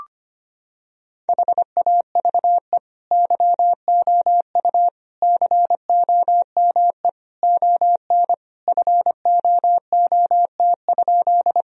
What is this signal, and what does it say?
Morse code, heard as a keyed tone, about 25 wpm, 700 Hz. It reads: HAVE YOU COME ON FOOT?
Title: Sentences from Top 1000 Words - Review Part 3 - 25wpm